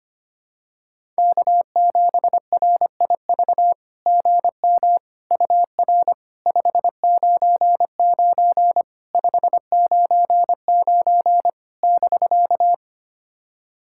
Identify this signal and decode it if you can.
Morse code, heard as a keyed tone, about 25 wpm, 690 Hz. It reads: K7RIV GM UR 599 599 <BK>
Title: SOTA QSO Practice 25wpm